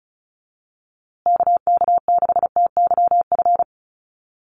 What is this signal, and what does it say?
Morse code, heard as a keyed tone, about 34 wpm, 700 Hz. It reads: KK6TYF
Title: USA Call Signs Repeated 2x 35wpm